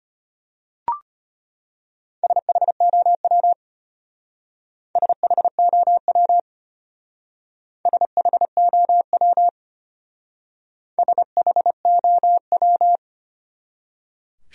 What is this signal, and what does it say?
Morse code, heard as a keyed tone, about 32 wpm, 700 Hz. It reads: SHOW SHOW SHOW SHOW